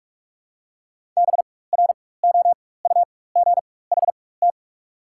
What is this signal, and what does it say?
Morse code, heard as a keyed tone, about 44 wpm, spaced as for 21 wpm, 690 Hz. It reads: BROUGHT
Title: 4x Character Spacing - Top 500 Words - 45wpm